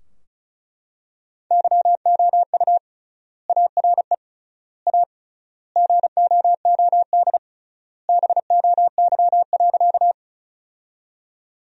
Morse code, heard as a keyed tone, about 34 wpm, 700 Hz. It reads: YOU ARE A GOOD BOY.
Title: 3x Word-Spacing - Sentences from Top 200 Words - 35wpm